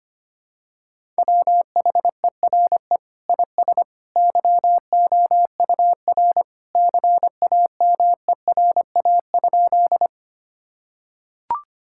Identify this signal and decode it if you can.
Morse code, heard as a keyed tone, about 25 wpm, 700 Hz. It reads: WHERE IS YOUR CAMERA?
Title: Sentences from Top 1000 Words - Review Part 5 - 25wpm